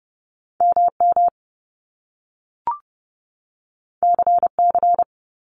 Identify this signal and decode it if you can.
Morse code, heard as a keyed tone, about 30 wpm, 700 Hz. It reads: MM CC
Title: Top Two Letter Combinations in Words 30wpm